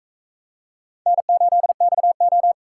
Morse code, heard as a keyed tone, about 41 wpm, 700 Hz. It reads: N8XO